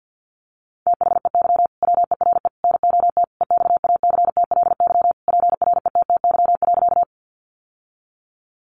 Morse code, heard as a keyed tone, about 50 wpm, 700 Hz. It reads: THEY WERE NOT EXACTLY PRETTY.
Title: Sentences from Top 1000 Words 50wpm